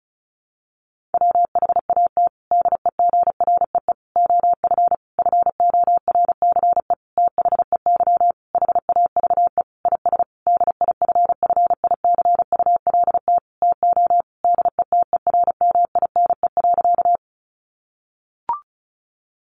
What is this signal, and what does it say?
Morse code, heard as a keyed tone, about 34 wpm, 700 Hz. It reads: WHAT DEGREE OF FORCE THEY HAVE IS DIFFICULT TO DETERMINE.
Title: Sentences from Top 1500 Words 35wpm